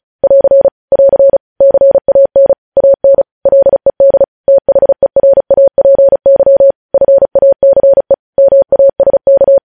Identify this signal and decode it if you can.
Morse code, heard as a keyed tone, about 36 wpm, 550 Hz. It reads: <AR> <AR> CAN AN LED THERAPY FACE MASK
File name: podcast / News headlines-1x-35x35